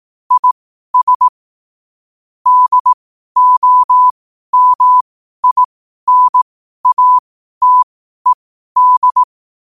Morse code, heard as a keyed tone, about 18 wpm, spaced as for 12 wpm, 1000 Hz. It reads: IS DOMINATED
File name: qotdfiles / 12wpm